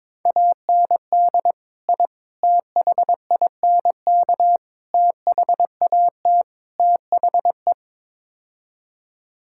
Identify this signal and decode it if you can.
Morse code, heard as a keyed tone, about 22 wpm, 700 Hz. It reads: AND I THINK THAT THE
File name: practice / 22wpm / Build A Phrase - 5-Word Phrases - 22wpm